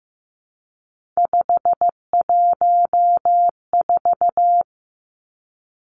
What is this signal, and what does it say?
Morse code, heard as a keyed tone, about 15 wpm, 700 Hz. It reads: 514
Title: Three Challenging Letter-Number Combinations 15wpm